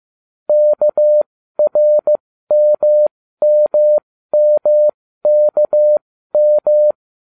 Morse code, heard as a keyed tone, about 15 wpm, spaced as for 13 wpm, 600 Hz. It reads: KRMMMKM